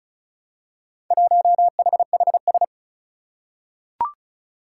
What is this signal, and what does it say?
Morse code, heard as a keyed tone, about 36 wpm, 700 Hz. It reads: 1HHS
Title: Four Challenging Letter-Number Combinations 35wpm